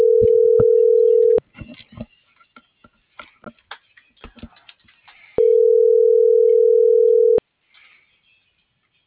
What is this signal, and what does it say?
Ringback tone (North America), 440 + 480 Hz.